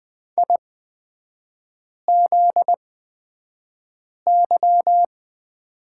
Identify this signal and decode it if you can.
Morse code, heard as a keyed tone, about 20 wpm, spaced as for 5 wpm, 700 Hz. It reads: IZY
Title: 250 Lesson 40e - BK - Three Characters 20wpm